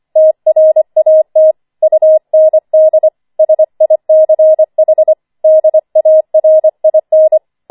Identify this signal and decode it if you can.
Morse code, heard as a keyed tone, about 25 wpm, 630 Hz. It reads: TRAT UND SICH DARIN
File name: 25wpm